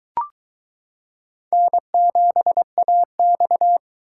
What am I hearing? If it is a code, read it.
Morse code, heard as a keyed tone, about 22 wpm, 700 Hz. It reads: N7AX